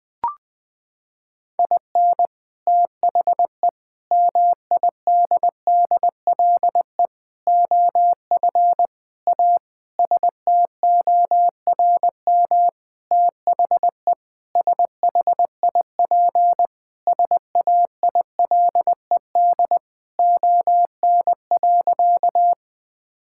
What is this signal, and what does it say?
Morse code, heard as a keyed tone, about 20 wpm, 700 Hz. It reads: IN THE MIDDLE OF A STORM THE SHIP SAILED ON.
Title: Sentences with 5-Word Phrases - 20wpm